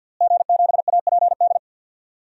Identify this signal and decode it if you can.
Morse code, heard as a keyed tone, about 47 wpm, 700 Hz. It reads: G7APD
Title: DXCC-100 Call Signs - 5 Characters - Repeated 3x - 50wpm